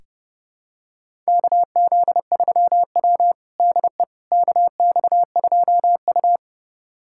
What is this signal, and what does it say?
Morse code, heard as a keyed tone, about 32 wpm, 700 Hz. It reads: KZ3W DE KX2U
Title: USA Callsign DE Callsign - 4 Characters - 30wpm